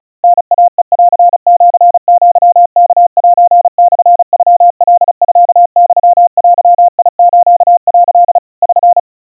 Morse code (Japanese Wabun code), heard as a keyed tone, about 36 wpm, 700 Hz. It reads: タイヘンシアワセモノカミユデス」 <SN>
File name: kawa / Morse sample / kamyu 130 cpm 700 Hz Crispy